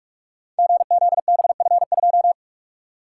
Morse code, heard as a keyed tone, about 44 wpm, 700 Hz. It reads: GZBF2
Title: Code Groups - 5 Letter:Numbers - 45wpm